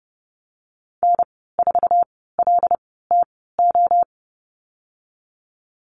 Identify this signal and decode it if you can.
Morse code, heard as a keyed tone, about 30 wpm, spaced as for 17 wpm, 700 Hz. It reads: N4LTO
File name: Lesson 14g - Number 2 - Call Signs 30wpm